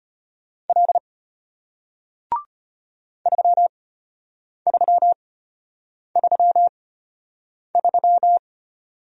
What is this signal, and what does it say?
Morse code, heard as a keyed tone, about 33 wpm, spaced as for 8 wpm, 700 Hz. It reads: L 3333